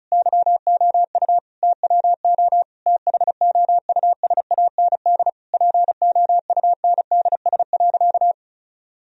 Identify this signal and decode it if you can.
Morse code, heard as a keyed tone, about 34 wpm, 700 Hz. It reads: YOU TWO THOUSAND POUNDS.
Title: Sentences from Top 1000 Words - Review Part 4 - 35wpm